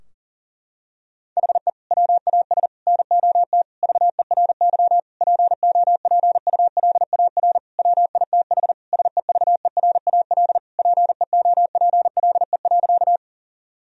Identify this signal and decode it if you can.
Morse code, heard as a keyed tone, about 37 wpm, 700 Hz. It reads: HE WAS NOT VERY POPULAR WITH SEVERAL PEOPLE.
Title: Sentences from Top 1000 Words - Review Part 3 - 40wpm